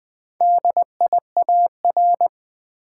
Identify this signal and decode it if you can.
Morse code, heard as a keyed tone, about 20 wpm, 700 Hz. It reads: DIAR
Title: Build A Word - 1 Letter At A Time - 5-Letter Words - 20wpm